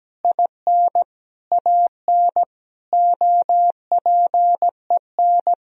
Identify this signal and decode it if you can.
Morse code, heard as a keyed tone, about 17 wpm, 700 Hz. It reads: IN AN OPEN
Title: Sentences from Top 1000 Words - Review Part 2 - 17wpm